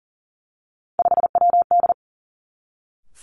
Morse code, heard as a keyed tone, about 42 wpm, 690 Hz. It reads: 5WD
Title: Three Challenging Letter-Number Combinations 40wpm